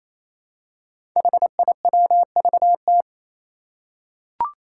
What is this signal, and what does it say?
Morse code, heard as a keyed tone, about 28 wpm, 700 Hz. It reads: HIWVT
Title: Five Challenging Letter-Number Combinations 28wpm